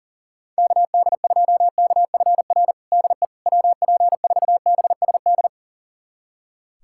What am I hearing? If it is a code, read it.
Morse code, heard as a keyed tone, about 40 wpm, 700 Hz. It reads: KD2KFR DE WP4BSD
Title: USA Callsign DE Callsign 40wpm